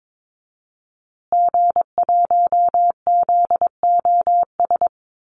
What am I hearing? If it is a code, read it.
Morse code, heard as a keyed tone, about 22 wpm, 700 Hz. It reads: G1ZOS
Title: Call Signs Repeated 2x - 22wpm